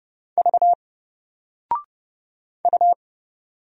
Morse code, heard as a keyed tone, about 30 wpm, 700 Hz. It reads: V U
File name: Single-Letter-Number-30wpm